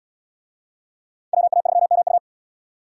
Morse code, heard as a keyed tone, about 70 wpm, 690 Hz. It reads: WA3MU